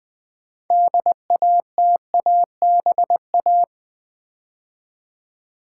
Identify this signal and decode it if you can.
Morse code, heard as a keyed tone, about 20 wpm, 700 Hz. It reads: DATABA
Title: Build A Word - 3 Letters At A Time - 8-Letter Words - 20wpm